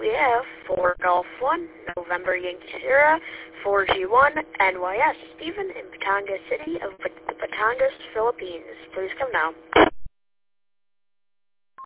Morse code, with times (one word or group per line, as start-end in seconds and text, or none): none